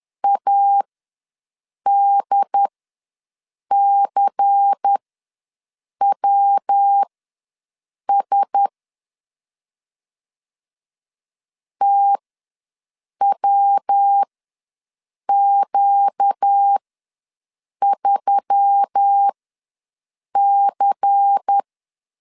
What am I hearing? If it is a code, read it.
Morse code, heard as a keyed tone, about 10 wpm, spaced as for 6 wpm, 800 Hz. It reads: ADCWS TWQ3C
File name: LEZIONI / 5 LBVC / 5 LBVC ws 10wpm 3733